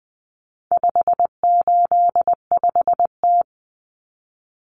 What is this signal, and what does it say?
Morse code, heard as a keyed tone, about 20 wpm, 700 Hz. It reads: 585T
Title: Common Cut Numbers - 4-Digit 20wpm